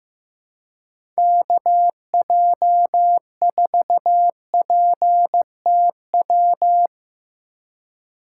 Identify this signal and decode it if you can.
Morse code, heard as a keyed tone, about 15 wpm, 700 Hz. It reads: KJ4PTW
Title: USA Call Signs - 6 Characters - 2x Character Spacing - 15wpm